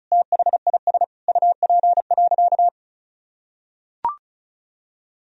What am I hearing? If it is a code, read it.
Morse code, heard as a keyed tone, about 36 wpm, 700 Hz. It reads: THIS UP.
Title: Sentences from Top 1000 Words - Review Part 1 - 35wpm